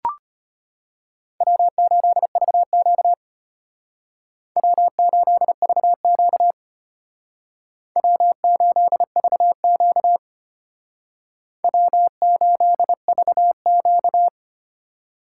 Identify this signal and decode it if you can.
Morse code, heard as a keyed tone, about 33 wpm, 700 Hz. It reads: W8VQ W8VQ W8VQ W8VQ